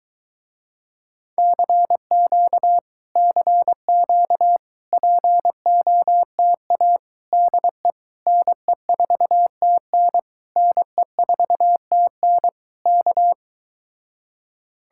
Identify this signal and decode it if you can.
Morse code, heard as a keyed tone, about 22 wpm, 700 Hz. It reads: CQ CQ POTA DE NE4TN NE4TN K